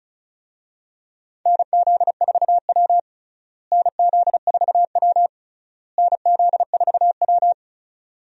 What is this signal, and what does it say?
Morse code, heard as a keyed tone, about 36 wpm, 700 Hz. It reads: NZ4W NZ4W NZ4W
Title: USA Call Signs - 4 Characters - Repeated 3x - 35wpm